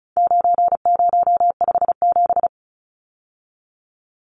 Morse code, heard as a keyed tone, about 34 wpm, 700 Hz. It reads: 9057
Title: Numbers - 4-Digit 35wpm